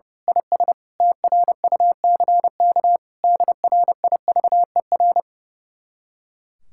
Morse code, heard as a keyed tone, about 28 wpm, 700 Hz. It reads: IS TRUCK DRIVER
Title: Speed-Racing Common QSO Phrases - 30wpm